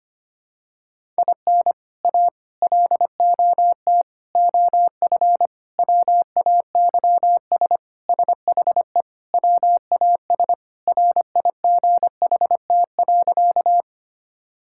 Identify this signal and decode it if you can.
Morse code, heard as a keyed tone, about 25 wpm, 700 Hz. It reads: IN A LOT OF WAYS SHE WAS RIGHT.
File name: practice / 25wpm / Sentences with 4-Word Phrases - 25wpm